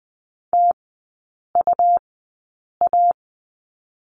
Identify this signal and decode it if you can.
Morse code, heard as a keyed tone, about 20 wpm, spaced as for 8 wpm, 700 Hz. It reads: TUA
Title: 100 Lesson 17e - Letter u - Three Characters 20wpm